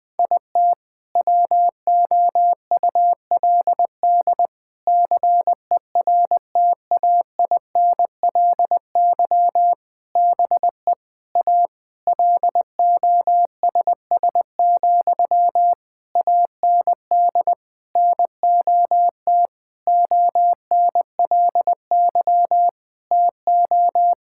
Morse code, heard as a keyed tone, about 21 wpm, 700 Hz. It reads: IT WOULD CERTAINLY BE A LOSS, AND NOT ONLY TO
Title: Sentences from Top 1600 Words 20wpm